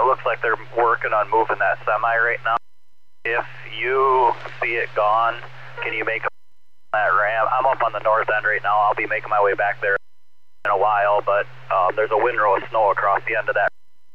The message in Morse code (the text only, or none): none